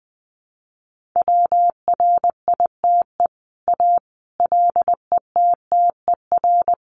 Morse code, heard as a keyed tone, about 20 wpm, 700 Hz. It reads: WRITE A LETTER
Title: Sentences from Top 1400 Words 20wpm